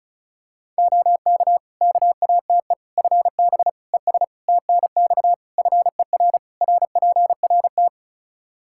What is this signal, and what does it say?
Morse code, heard as a keyed tone, about 34 wpm, 700 Hz. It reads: OK KATE FB ES TNX FER RPRT
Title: Practice QSO - Exchange 2 - Info - 35wpm